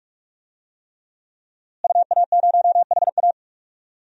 Morse code, heard as a keyed tone, about 46 wpm, 700 Hz. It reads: UA0HA
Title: DXCC-100 Call Signs - 5 Characters - Repeated 3x - 45wpm